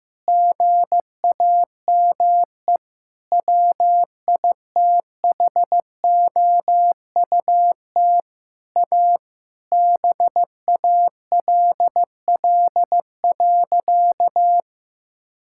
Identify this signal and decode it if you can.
Morse code, heard as a keyed tone, about 15 wpm, 700 Hz. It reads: GAME WITHOUT A BALL.
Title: Sentences from Top 1000 Words - Review Part 6 - 15wpm